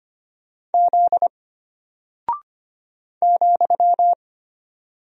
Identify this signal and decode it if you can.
Morse code, heard as a keyed tone, about 25 wpm, 700 Hz. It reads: Z ,
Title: Single Character (Letter:Number:Punctuation:Prosign) Alt 25wpm